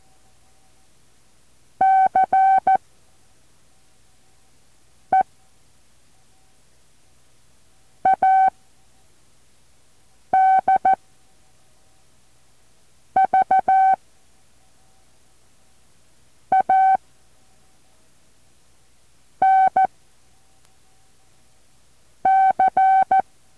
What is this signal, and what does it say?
Morse code, heard as a keyed tone, about 14 wpm, 760 Hz. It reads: C E A D V A N C
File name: intl morse code2